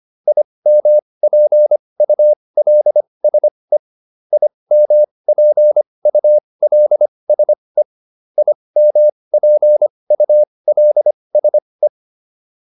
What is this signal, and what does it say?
Morse code, heard as a keyed tone, about 25 wpm, spaced as for 20 wpm, 600 Hz. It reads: IMPULSE IMPULSE IMPULSE